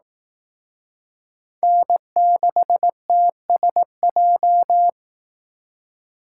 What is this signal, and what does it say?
Morse code, heard as a keyed tone, about 19 wpm, 700 Hz. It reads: N6TSJ